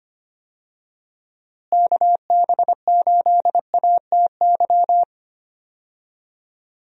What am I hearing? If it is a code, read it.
Morse code, heard as a keyed tone, about 25 wpm, 700 Hz. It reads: KB8ATY
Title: USA Call Signs - 6 Characters - Repeated 2x - 25wpm